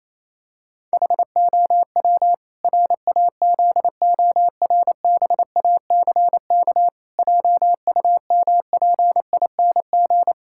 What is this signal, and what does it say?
Morse code, heard as a keyed tone, about 28 wpm, 700 Hz. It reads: HOW RAZORBACK JUMPING
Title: Pangram Challenge v2 - 28wpm